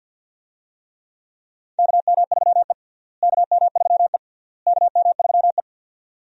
Morse code, heard as a keyed tone, about 52 wpm, 700 Hz. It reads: KM3E KM3E KM3E
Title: USA Call Signs - 4 Characters - Repeated 3x - 50wpm